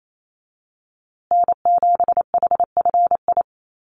Morse code, heard as a keyed tone, about 28 wpm, 700 Hz. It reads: N7HFI
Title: USA Call Signs Repeated 2x 28wpm